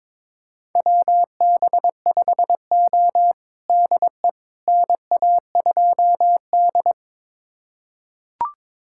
Morse code, heard as a keyed tone, about 21 wpm, 700 Hz. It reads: WB5O DE NA2D
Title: USA Callsign DE Callsign - 4 Characters - 22wpm